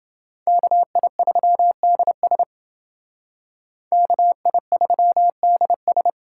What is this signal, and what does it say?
Morse code, heard as a keyed tone, about 28 wpm, 700 Hz. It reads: KI3DS KI3DS